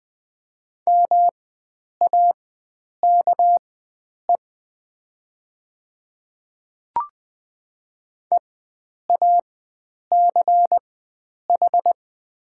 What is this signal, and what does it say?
Morse code, heard as a keyed tone, about 20 wpm, spaced as for 9 wpm, 700 Hz. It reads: MAKE EACH
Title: 4x Character Spacing - Top 100 Words - 20wpm